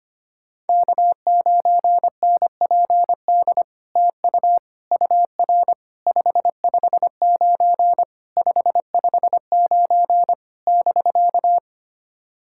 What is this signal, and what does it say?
Morse code, heard as a keyed tone, about 25 wpm, 700 Hz. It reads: K9NPD TU UR 559 559 <BK>
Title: SOTA QSO Practice 25wpm